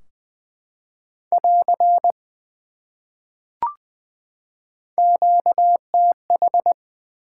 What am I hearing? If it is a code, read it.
Morse code, heard as a keyed tone, about 20 wpm, 700 Hz. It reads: <AR> QTH
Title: Top 180 QSO Elements 20wpm